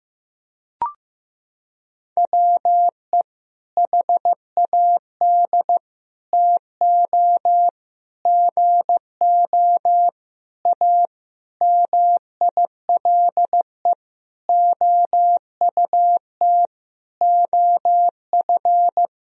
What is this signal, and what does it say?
Morse code, heard as a keyed tone, about 15 wpm, 700 Hz. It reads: WE HAD TO GO A MILE OUT OF
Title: Sentences from Top 1000 Words - Review Part 1 - 15wpm